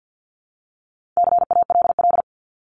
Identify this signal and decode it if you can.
Morse code, heard as a keyed tone, about 48 wpm, 700 Hz. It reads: CALL
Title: Top 300 Words 50wpm